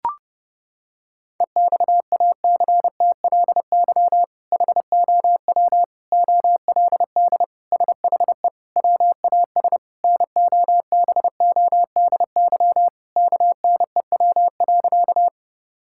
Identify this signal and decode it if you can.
Morse code, heard as a keyed tone, about 31 wpm, 700 Hz. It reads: EXACTLY HOW OLD SHE WAS NOBODY KNEW.
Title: Sentences from Top 1600 Words 30wpm